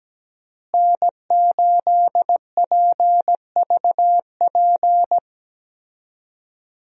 Morse code, heard as a keyed tone, about 17 wpm, 700 Hz. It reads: N8PVP
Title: USA Call Signs - 5 Characters - Repeated 3x - 17wpm